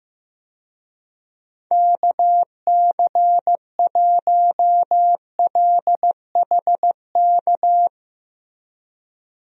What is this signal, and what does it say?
Morse code, heard as a keyed tone, about 15 wpm, 700 Hz. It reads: KC1LHK
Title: USA Call Signs - 6 Characters - Repeated 3x - 15wpm